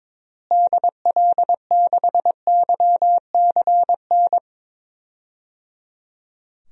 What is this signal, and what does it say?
Morse code, heard as a keyed tone, about 22 wpm, 700 Hz. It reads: DL6YCN
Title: DXCC-100 Call Signs - 6 Characters - Repeated 3x - 22wpm